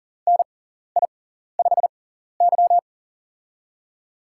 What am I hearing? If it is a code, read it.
Morse code, heard as a keyed tone, about 40 wpm, spaced as for 14 wpm, 700 Hz. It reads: NI5Y
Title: USA Call Signs - 4 Characters - 6x Character Spacing - 40wpm